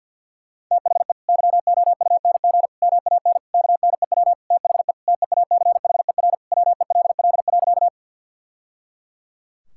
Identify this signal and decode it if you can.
Morse code, heard as a keyed tone, about 50 wpm, 700 Hz. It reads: THE YOUNG MAN KNEW THE TEACHER WELL.
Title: Sentences from Top 1000 Words - Review Part 6 - 50wpm